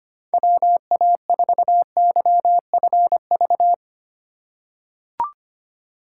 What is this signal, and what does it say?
Morse code, heard as a keyed tone, about 25 wpm, 700 Hz. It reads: WA4YFV
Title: USA Call Signs - 6 Characters - 2x Character Spacing - 25wpm